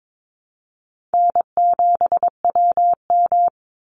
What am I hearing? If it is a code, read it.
Morse code, heard as a keyed tone, about 22 wpm, 700 Hz. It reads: N7WM
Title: NA SOTA Chaser Callsigns 22wpm